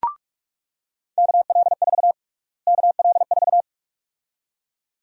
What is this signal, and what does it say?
Morse code, heard as a keyed tone, about 42 wpm, 700 Hz. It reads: KL4 KL4
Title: USA Call Sign Prefixes - Repeated 2x - 45wpm